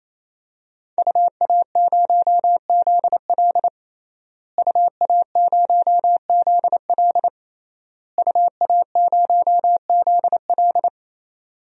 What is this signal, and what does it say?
Morse code, heard as a keyed tone, about 28 wpm, 700 Hz. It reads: UA0ZL UA0ZL UA0ZL
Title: DXCC-100 Call Signs - 5 Characters - Repeated 3x - 28wpm